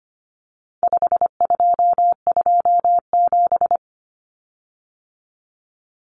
Morse code, heard as a keyed tone, about 25 wpm, 700 Hz. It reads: 5227
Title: Common Cut Numbers - 4-Digit 25wpm